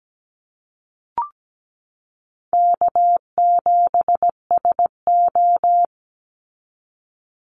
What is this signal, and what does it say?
Morse code, heard as a keyed tone, about 17 wpm, 700 Hz. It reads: K7SO
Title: NA SOTA Activator Callsigns 17wpm